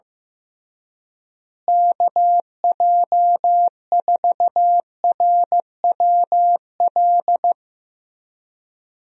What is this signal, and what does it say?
Morse code, heard as a keyed tone, about 16 wpm, 700 Hz. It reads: KJ4RWL
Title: USA Call Signs - 6 Characters - Repeated 3x - 15wpm